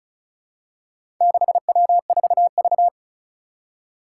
Morse code, heard as a keyed tone, about 36 wpm, 700 Hz. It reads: 6W4V
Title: Four Challenging Letter-Number Combinations 35wpm